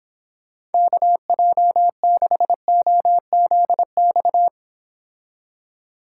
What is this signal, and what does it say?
Morse code, heard as a keyed tone, about 30 wpm, 700 Hz. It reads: KJ6OZX